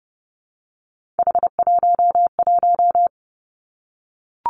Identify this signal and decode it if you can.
Morse code, heard as a keyed tone, about 30 wpm, 700 Hz. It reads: H11
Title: Three Challenging Letter-Number Combinations 30wpm